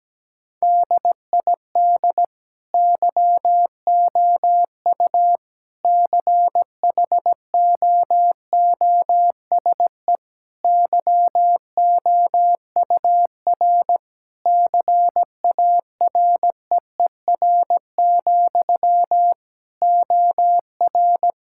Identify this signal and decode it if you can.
Morse code, heard as a keyed tone, about 17 wpm, 700 Hz. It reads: DID YOU CHOOSE YOUR CAREER, OR
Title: Sentences from Top 1000 Words - Review Part 6 - 17wpm